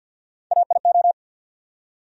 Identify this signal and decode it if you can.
Morse code, heard as a keyed tone, about 50 wpm, 700 Hz. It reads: AIO
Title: USA Call Sign Postfixes - 3 Letters 50wpm